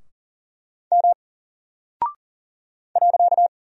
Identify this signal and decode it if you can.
Morse code, heard as a keyed tone, about 39 wpm, 700 Hz. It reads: M .
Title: Single Character (Letter:Number:Punctuation:Prosign) 40wpm